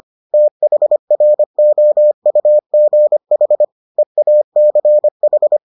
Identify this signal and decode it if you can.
Morse code, heard as a keyed tone, about 25 wpm, 600 Hz. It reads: THROUGH EACH